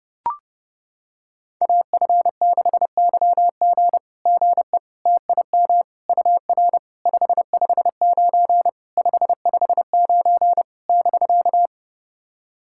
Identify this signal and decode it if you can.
Morse code, heard as a keyed tone, about 30 wpm, 690 Hz. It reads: AF6YG GE TIM UR 559 559 <BK>
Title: SOTA QSO Practice 30wpm